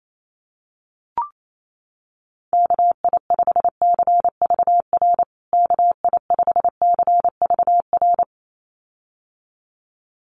Morse code, heard as a keyed tone, about 28 wpm, 700 Hz. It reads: KI5CVR KI5CVR
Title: USA Call Signs Repeated 2x 28wpm